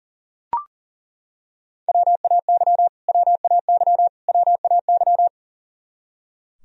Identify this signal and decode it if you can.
Morse code, heard as a keyed tone, about 40 wpm, 700 Hz. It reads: WAY WAY WAY
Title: Top 100 Words Repeated 40wpm